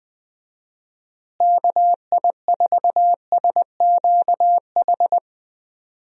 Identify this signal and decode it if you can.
Morse code, heard as a keyed tone, about 20 wpm, 700 Hz. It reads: KI4SQH
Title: USA Call Signs - 6 Characters - Repeated 3x - 20wpm